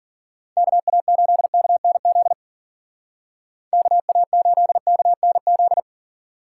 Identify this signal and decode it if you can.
Morse code, heard as a keyed tone, about 50 wpm, 700 Hz. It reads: KA8KNZ KA8KNZ